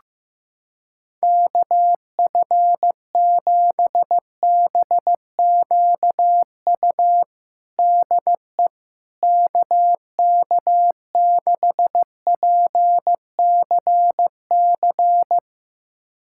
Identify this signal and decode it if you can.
Morse code, heard as a keyed tone, about 15 wpm, 700 Hz. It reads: KF7BQU DE KK6PCC
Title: USA Callsign DE Callsign 15wpm